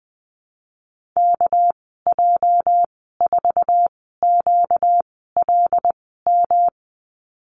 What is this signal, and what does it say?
Morse code, heard as a keyed tone, about 20 wpm, spaced as for 14 wpm, 700 Hz. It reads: KJ4QLM
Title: 238 Lesson 38g - Letter q - Call Signs 20wpm